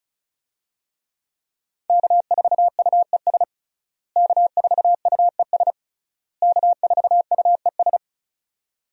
Morse code, heard as a keyed tone, about 36 wpm, 700 Hz. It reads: K4UES K4UES K4UES
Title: USA Call Signs - 5 Characters - Repeated 3x - 35wpm